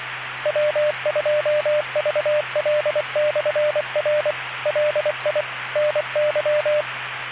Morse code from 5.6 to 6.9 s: NY